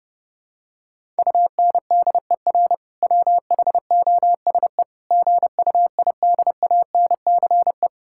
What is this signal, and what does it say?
Morse code, heard as a keyed tone, about 32 wpm, 700 Hz. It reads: UNDER WHOSE GUIDANCE
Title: Sentences from Top 1600 Words 30wpm